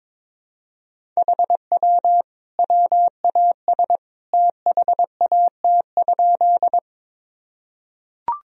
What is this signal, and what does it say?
Morse code, heard as a keyed tone, about 22 wpm, 700 Hz. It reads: HW WAS THAT?
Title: Sentences from Top 100 Words and Abbreviations 22wpm